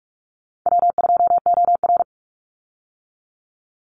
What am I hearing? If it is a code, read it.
Morse code, heard as a keyed tone, about 46 wpm, 700 Hz. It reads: W2OR
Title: Easy Call Signs - 45wpm